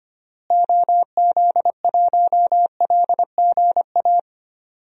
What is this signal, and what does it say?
Morse code, heard as a keyed tone, about 25 wpm, 700 Hz. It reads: OZ1LGA
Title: DXCC-100 Call Signs - 6 Characters - Repeated 2x - 25wpm